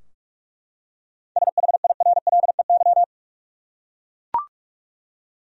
Morse code, heard as a keyed tone, about 42 wpm, 700 Hz. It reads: SHIRLEY
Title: USA Common Names - 45wpm